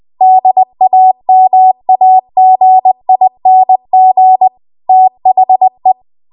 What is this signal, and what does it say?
Morse code, heard as a keyed tone, about 20 wpm, 750 Hz. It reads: DAMAGING THE
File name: Archive / 20 WPM / 20WPM